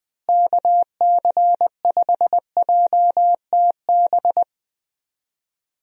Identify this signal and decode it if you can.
Morse code, heard as a keyed tone, about 20 wpm, 700 Hz. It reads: KC5JTB
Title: USA Call Signs - 6 Characters - Repeated 3x - 20wpm